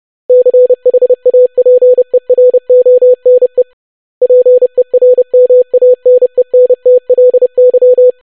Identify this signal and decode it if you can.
Morse code, heard as a keyed tone, about 30 wpm, 500 Hz. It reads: CHAPERONE PERMANENTLY
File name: longwords-25wpm